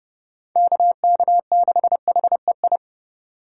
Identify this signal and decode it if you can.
Morse code, heard as a keyed tone, about 37 wpm, 700 Hz. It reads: KK6HEI
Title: Speed Racing - USA Call Signs 30wpm